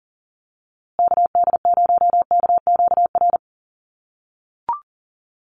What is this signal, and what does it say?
Morse code, heard as a keyed tone, about 40 wpm, 700 Hz. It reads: KD0KQR
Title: USA Call Signs Repeated 2x 40wpm